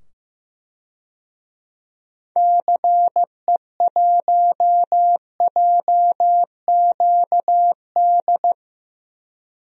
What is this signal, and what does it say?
Morse code, heard as a keyed tone, about 15 wpm, 700 Hz. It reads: CE1JQD
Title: DXCC-100 Call Signs - 6 Characters - 15wpm